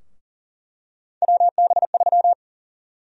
Morse code, heard as a keyed tone, about 38 wpm, 700 Hz. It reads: WB3
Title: USA Call Sign Prefixes - 40wpm